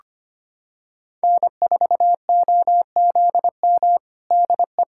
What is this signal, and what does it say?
Morse code, heard as a keyed tone, about 25 wpm, 700 Hz. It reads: N4OZM DE
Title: USA Callsign DE Callsign - 5 Characters - 25wpm